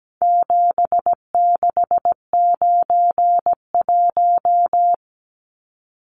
Morse code, heard as a keyed tone, about 17 wpm, 700 Hz. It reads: 7691
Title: Numbers - 4-Digit 17wpm